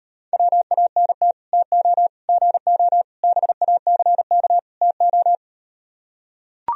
Morse code, heard as a keyed tone, about 38 wpm, 700 Hz. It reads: WANT TO GO BACK TO